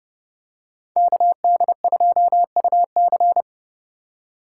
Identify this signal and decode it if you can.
Morse code, heard as a keyed tone, about 30 wpm, 700 Hz. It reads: KD2UC